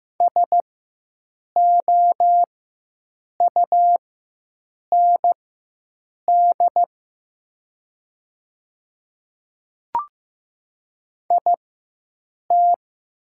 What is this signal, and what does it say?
Morse code, heard as a keyed tone, about 15 wpm, spaced as for 7 wpm, 700 Hz. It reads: SOUND IT
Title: 4x Character Spacing - Top 100 Words - 15wpm